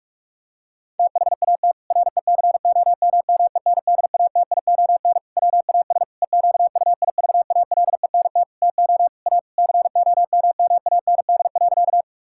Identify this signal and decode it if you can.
Morse code, heard as a keyed tone, about 46 wpm, 700 Hz. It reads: THAT RECOMMENDATION WAS EQUIVALENT TO A COMMAND.
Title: Sentences from Top 1600 Words 45wpm